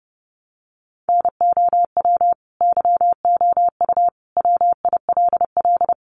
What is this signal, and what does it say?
Morse code, heard as a keyed tone, about 30 wpm, 700 Hz. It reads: NOW YOU WILL
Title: Sentences from Top 1200 Words 30wpm